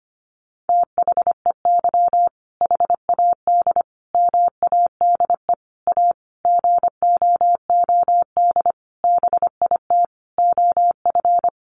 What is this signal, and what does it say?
Morse code, heard as a keyed tone, about 25 wpm, 700 Hz. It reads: THEY HAD MADE A GOOD BIT OF
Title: Sentences from Top 900 Words 25wpm